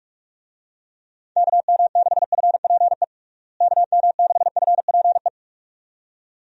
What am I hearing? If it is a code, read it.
Morse code, heard as a keyed tone, about 43 wpm, 700 Hz. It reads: KM6FPE KM6FPE
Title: USA Call Signs - 6 Characters - Repeated 2x - 45wpm